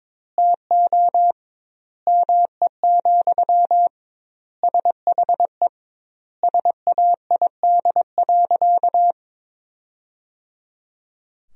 Morse code, heard as a keyed tone, about 22 wpm, 700 Hz. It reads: TO ME, SHE SAID.
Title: 2x Word-Spacing - Sentences from Top 500 Words - 22wpm